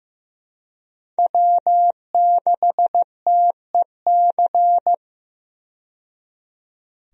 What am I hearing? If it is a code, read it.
Morse code, heard as a keyed tone, about 15 wpm, 700 Hz. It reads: W6TEC
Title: USA Call Signs - 5 Characters - 15wpm